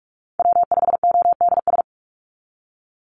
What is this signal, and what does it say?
Morse code, heard as a keyed tone, about 46 wpm, 700 Hz. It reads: W5ODS
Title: NA SOTA Activator Callsigns 45wpm